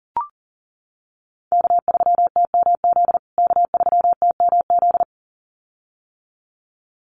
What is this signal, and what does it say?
Morse code, heard as a keyed tone, about 40 wpm, 700 Hz. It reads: K3TMZ K3TMZ
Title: USA Call Signs Repeated 2x 40wpm